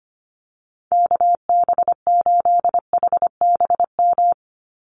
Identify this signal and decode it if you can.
Morse code, heard as a keyed tone, about 25 wpm, 700 Hz. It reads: KB8HBM
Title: USA Call Signs 25wpm